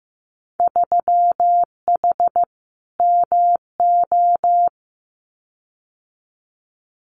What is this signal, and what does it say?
Morse code, heard as a keyed tone, about 15 wpm, 700 Hz. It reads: 3H MO
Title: Winter Field Day Exchanges Encore 15wpm